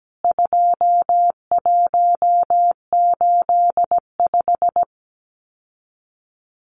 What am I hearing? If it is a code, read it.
Morse code, heard as a keyed tone, about 17 wpm, 700 Hz. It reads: 2185
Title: CWT Exchange 17wpm